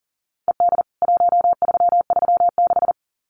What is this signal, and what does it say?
Morse code, heard as a keyed tone, about 40 wpm, 700 Hz. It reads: ED 1336
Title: CWT Exchange 40wpm